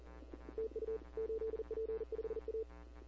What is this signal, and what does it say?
Morse code, heard as a keyed tone, about 40 wpm, 440 Hz. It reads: TU 8P5A
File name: CW / 2005-k5tr arrl cw dx -15:00